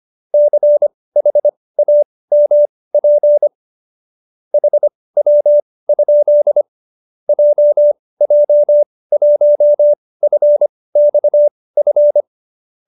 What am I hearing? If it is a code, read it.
Morse code, heard as a keyed tone, about 25 wpm, spaced as for 18 wpm, 590 Hz. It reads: CHAMP HW? JJ1FXF